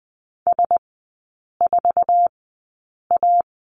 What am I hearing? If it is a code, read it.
Morse code, heard as a keyed tone, about 20 wpm, spaced as for 8 wpm, 700 Hz. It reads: S4A
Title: 053 Lesson 10e - Letter r - Three Characters 20wpm